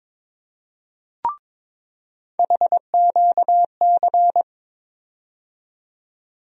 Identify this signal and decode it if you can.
Morse code, heard as a keyed tone, about 21 wpm, 700 Hz. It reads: HQC
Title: USA Call Sign Postfixes - 3 Letters 22wpm